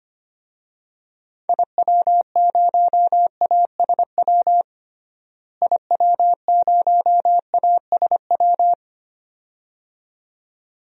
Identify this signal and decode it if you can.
Morse code, heard as a keyed tone, about 25 wpm, 700 Hz. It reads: IW0ASW IW0ASW
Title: DXCC-100 Call Signs - 6 Characters - Repeated 2x - 25wpm